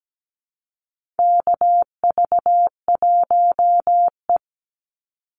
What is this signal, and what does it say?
Morse code, heard as a keyed tone, about 17 wpm, 700 Hz. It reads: KV1E
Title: Call Signs - 17wpm